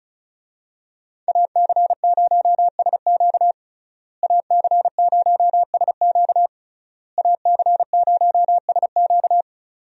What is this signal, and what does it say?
Morse code, heard as a keyed tone, about 34 wpm, 700 Hz. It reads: AC0SQ AC0SQ AC0SQ
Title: USA Call Signs - 5 Characters - Repeated 3x - 35wpm